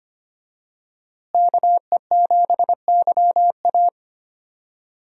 Morse code, heard as a keyed tone, about 25 wpm, 700 Hz. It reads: KE7YA
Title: USA Call Signs - 5 Characters - Repeated 3x - 25wpm